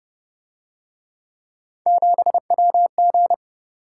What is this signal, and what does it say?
Morse code, heard as a keyed tone, about 29 wpm, 700 Hz. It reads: 7WG